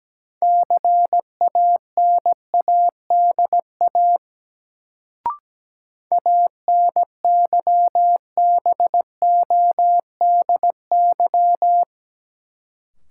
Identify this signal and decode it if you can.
Morse code, heard as a keyed tone, about 17 wpm, 700 Hz. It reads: CANADA ANYBODY
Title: Sets of 5 Words - New Word Drill - 17wpm